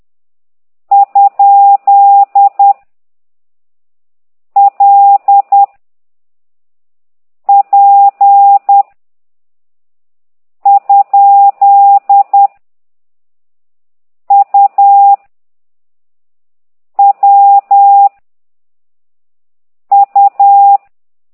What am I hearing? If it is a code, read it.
Morse code, heard as a keyed tone, about 10 wpm, 800 Hz. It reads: ? L P ? U W U
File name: ct2idw / exerc8